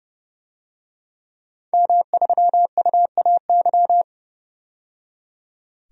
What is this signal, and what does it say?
Morse code, heard as a keyed tone, about 30 wpm, 700 Hz. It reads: M3UAY
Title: DXCC-100 Call Signs - 5 Characters - 30wpm